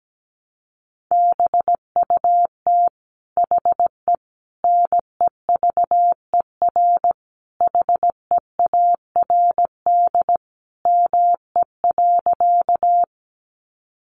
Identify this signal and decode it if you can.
Morse code, heard as a keyed tone, about 17 wpm, 700 Hz. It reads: BUT HE NEVER HEARD ME.
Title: Sentences from Top 400 Words 17wpm